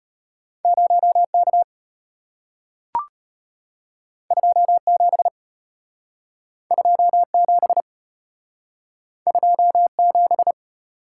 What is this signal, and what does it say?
Morse code, heard as a keyed tone, about 33 wpm, 690 Hz. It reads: 0K 27 27 27